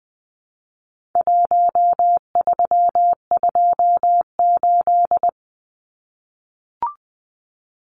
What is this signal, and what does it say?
Morse code, heard as a keyed tone, about 20 wpm, 700 Hz. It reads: 1328
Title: Numbers - Contest Style - 4-Digit 20wpm